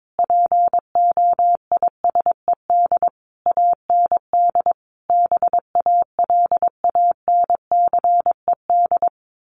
Morse code, heard as a keyed tone, about 22 wpm, 700 Hz. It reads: POISED AND BALANCED
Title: A Sound Of Thunder 22wpm